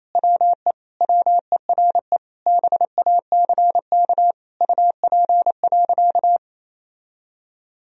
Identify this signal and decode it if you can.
Morse code, heard as a keyed tone, about 28 wpm, 700 Hz. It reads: WE WERE BACK UP.
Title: Sentences from Top 1000 Words - Review Part 2 - 28wpm